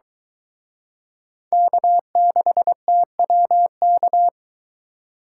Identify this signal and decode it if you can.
Morse code, heard as a keyed tone, about 22 wpm, 700 Hz. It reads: K6TWK